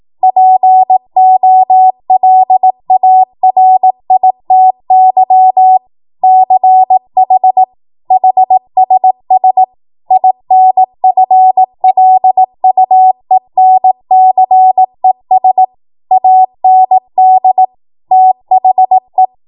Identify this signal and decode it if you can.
Morse code, heard as a keyed tone, about 18 wpm, 750 Hz. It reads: POLARITY CH HSS INFLUENCES AND THE
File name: Morse / 18WPM